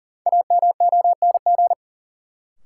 Morse code, heard as a keyed tone, about 37 wpm, 700 Hz. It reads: AMONG
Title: Sets of 2 Words - New Word Drill - 40wpm